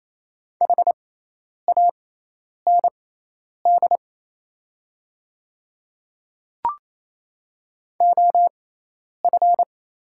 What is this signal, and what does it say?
Morse code, heard as a keyed tone, about 28 wpm, spaced as for 10 wpm, 700 Hz. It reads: HAND OF
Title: 6x Character Spacing - Top 200 Words - 28wpm